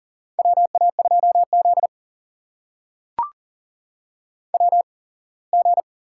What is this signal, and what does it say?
Morse code, heard as a keyed tone, about 40 wpm, spaced as for 24 wpm, 700 Hz. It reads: WA2Z WG
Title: USA Call Signs - 4 Characters - 8x Character Spacing - 40wpm